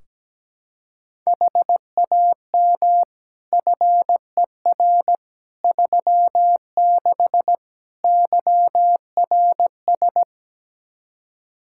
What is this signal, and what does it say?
Morse code, heard as a keyed tone, about 17 wpm, 700 Hz. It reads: HAM FER 36 YRS
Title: Practice QSO - Exchange 3 - More Info - 17wpm